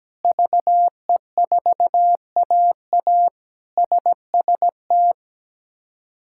Morse code, heard as a keyed tone, about 17 wpm, 700 Hz. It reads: VE4AA SST
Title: K1USN SST QSO Practice - 17wpm